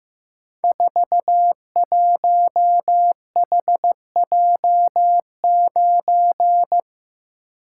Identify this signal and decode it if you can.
Morse code, heard as a keyed tone, about 15 wpm, 700 Hz. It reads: 41HJ9
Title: Code Groups - 5 Letter:Numbers - 15wpm